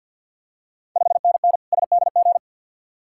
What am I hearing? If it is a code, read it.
Morse code, heard as a keyed tone, about 52 wpm, 700 Hz. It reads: 5NN SDG
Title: Ontario QSO Party Exchange 50wpm